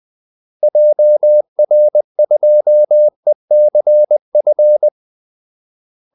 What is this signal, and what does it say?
Morse code, heard as a keyed tone, about 20 wpm, 600 Hz. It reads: JR2ECF
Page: Kaldesignaler 20 wpm